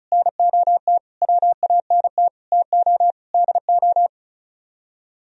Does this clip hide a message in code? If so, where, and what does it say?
Yes: Morse code from 0.1 to 4.1 s, heard as a keyed tone, about 45 wpm, 700 Hz. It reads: NOT WANT TO DO